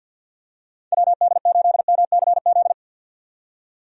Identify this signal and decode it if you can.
Morse code, heard as a keyed tone, about 50 wpm, 700 Hz. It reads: WD8MCZ
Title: USA Call Signs - 6 Characters - 50wpm